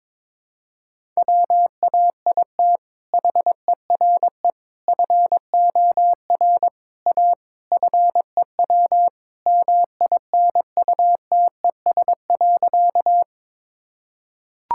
Morse code, heard as a keyed tone, about 22 wpm, 700 Hz. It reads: WAIT HERE FOR A FEW MINUTES.
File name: practice / 22wpm / Sentences with 3-Word Phrases - 22wpm